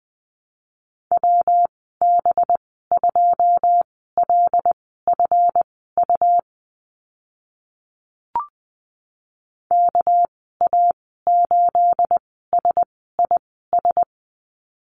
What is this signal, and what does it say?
Morse code, heard as a keyed tone, about 20 wpm, spaced as for 14 wpm, 700 Hz. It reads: WB2LFU KA8SIS
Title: 213 Lesson 34g - Number 8 - Call Signs 20wpm